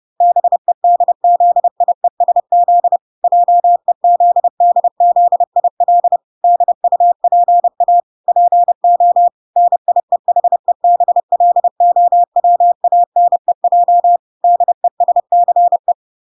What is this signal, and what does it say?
Morse code, heard as a keyed tone, about 30 wpm, 700 Hz. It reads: BEDZIESZ JEZDZIL DUPA PO NIEHEBLOWANEJ DESCE